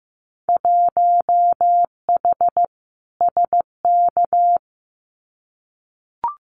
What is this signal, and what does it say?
Morse code, heard as a keyed tone, about 15 wpm, 700 Hz. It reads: 1H SK
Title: Winter Field Day Exchanges 15wpm